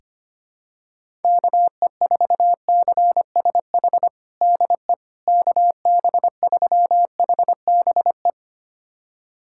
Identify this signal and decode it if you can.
Morse code, heard as a keyed tone, about 25 wpm, 700 Hz. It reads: KE4CSH DE KB3HBE
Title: USA Callsign DE Callsign 25wpm